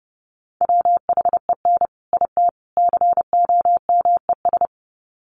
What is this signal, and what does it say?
Morse code, heard as a keyed tone, about 30 wpm, 700 Hz. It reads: WHEN IT COMES
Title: Sets of 3 Words 30wpm